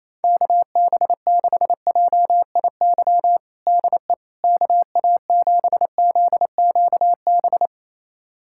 Morse code, heard as a keyed tone, about 28 wpm, 700 Hz. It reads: KB6JIY DE KA7ZQB
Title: USA Callsign DE Callsign - 6 Characters - 28wpm